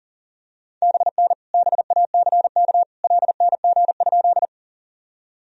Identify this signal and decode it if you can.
Morse code, heard as a keyed tone, about 37 wpm, 700 Hz. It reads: BN BACK LNG?
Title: Sentences from Top 200 Words and Abbreviations 40wpm